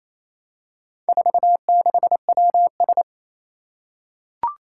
Morse code, heard as a keyed tone, about 28 wpm, 700 Hz. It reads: 46WS
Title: Four Challenging Letter-Number Combinations 28wpm